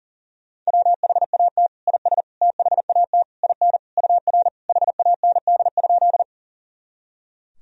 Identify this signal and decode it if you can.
Morse code, heard as a keyed tone, about 40 wpm, 700 Hz. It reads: WHAT IS THAT IN UR HAND?
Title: Sentences from Top 200 Words and Abbreviations 40wpm